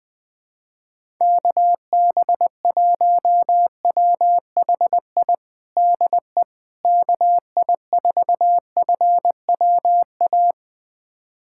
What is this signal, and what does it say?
Morse code, heard as a keyed tone, about 21 wpm, 700 Hz. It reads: KB1WHI DE KI4FWA
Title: USA Callsign DE Callsign - 6 Characters - 20wpm